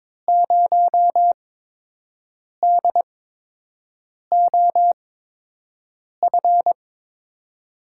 Morse code, heard as a keyed tone, about 22 wpm, spaced as for 6 wpm, 700 Hz. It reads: 0DOF
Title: USA Call Signs - 6 Characters - 8x Character Spacing - 22wpm